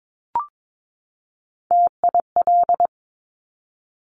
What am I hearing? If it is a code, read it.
Morse code, heard as a keyed tone, about 22 wpm, 690 Hz. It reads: TIL
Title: Top Three Letter Combinations in Words 22wpm